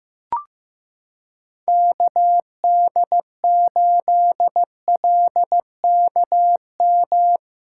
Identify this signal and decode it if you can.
Morse code, heard as a keyed tone, about 15 wpm, 700 Hz. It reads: KD8LKM
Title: USA Callsign DE Callsign - 6 Characters - 15wpm